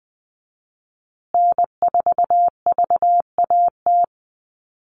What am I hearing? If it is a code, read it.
Morse code, heard as a keyed tone, about 20 wpm, 700 Hz. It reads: N4VAT
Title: USA Call Signs Repeated 3x 20wpm